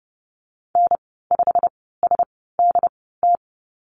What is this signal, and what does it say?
Morse code, heard as a keyed tone, about 30 wpm, spaced as for 17 wpm, 700 Hz. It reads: N5SDT
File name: Lesson 19g - Letter w - Call Signs 30wpm